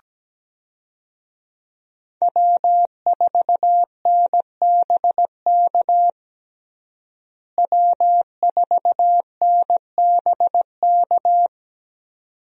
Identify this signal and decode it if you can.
Morse code, heard as a keyed tone, about 17 wpm, 700 Hz. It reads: W4NBK W4NBK
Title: USA Call Signs - 5 Characters - Repeated 2x - 17wpm